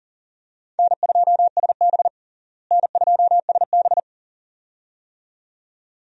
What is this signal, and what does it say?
Morse code, heard as a keyed tone, about 37 wpm, 700 Hz. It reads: N2SB N2SB
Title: USA Call Signs - 4 Characters - Repeated 2x - 40wpm